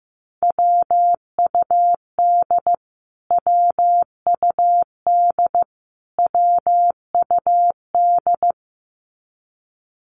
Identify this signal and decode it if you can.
Morse code, heard as a keyed tone, about 15 wpm, 700 Hz. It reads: WUD WUD WUD
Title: Abbreviations Repeated 3x - 15wpm